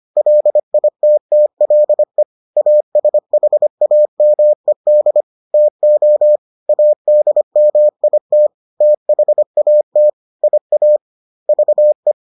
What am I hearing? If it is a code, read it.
Morse code, heard as a keyed tone, about 25 wpm, 600 Hz. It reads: LITTLE ASHAMED TO ADMIT THAT IA VE